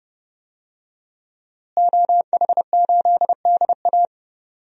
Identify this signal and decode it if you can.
Morse code, heard as a keyed tone, about 30 wpm, 700 Hz. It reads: OH8DA
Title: DXCC-100 Call Signs - 5 Characters - Repeated 2x - 30wpm